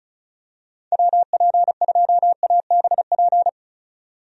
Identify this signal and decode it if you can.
Morse code, heard as a keyed tone, about 34 wpm, 700 Hz. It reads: WP2ABP
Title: USA Call Signs - 6 Characters - 35wpm